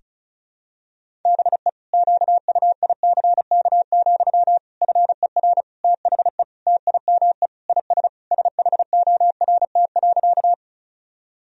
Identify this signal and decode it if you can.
Morse code, heard as a keyed tone, about 36 wpm, 700 Hz. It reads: BE QUICK, FER THE TIME IS SHORT.
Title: Sentences from Top 500 Words and Abbreviations 35wpm